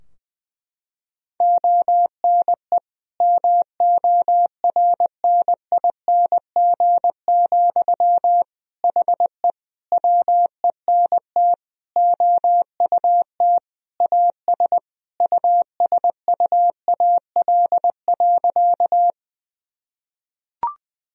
Morse code, heard as a keyed tone, about 20 wpm, 700 Hz. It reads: ONE MORNING, HE WENT OUT AS USUAL.
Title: Sentences from Top 1000 Words - Review Part 2 - 20wpm